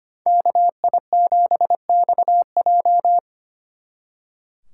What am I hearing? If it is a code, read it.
Morse code, heard as a keyed tone, about 30 wpm, 700 Hz. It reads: KI7XJ
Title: Speed Racing - USA 2x2 Call Signs 25wpm